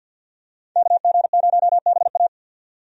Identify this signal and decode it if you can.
Morse code, heard as a keyed tone, about 52 wpm, 700 Hz. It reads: KG0BA
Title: USA Call Signs - 5 Characters - 50wpm